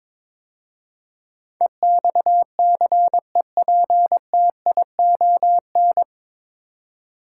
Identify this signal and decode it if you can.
Morse code, heard as a keyed tone, about 22 wpm, 700 Hz. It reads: EXCEPTION
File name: 9 Letter Words - First 3 Letters Given - 22wpm